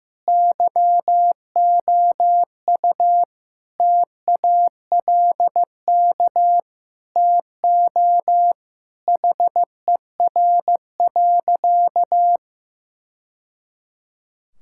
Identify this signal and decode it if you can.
Morse code, heard as a keyed tone, about 15 wpm, 700 Hz. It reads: YOU TALK TO HER.
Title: Sentences from Top 1000 Words - Review Part 6 - 15wpm